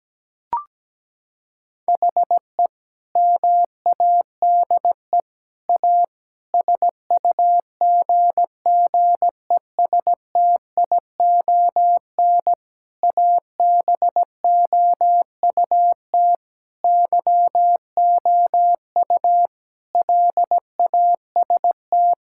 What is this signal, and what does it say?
Morse code, heard as a keyed tone, about 17 wpm, 700 Hz. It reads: HE MADE A SUGGESTION ABOUT YOU LAST
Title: Sentences from Top 1600 Words 17wpm